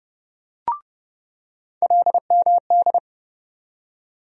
Morse code, heard as a keyed tone, about 30 wpm, 700 Hz. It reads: LMD
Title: USA Call Sign Postfixes - 3 Letters 30wpm